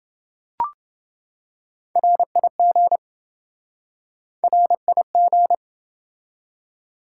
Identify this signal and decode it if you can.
Morse code, heard as a keyed tone, about 27 wpm, 700 Hz. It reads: RIG RIG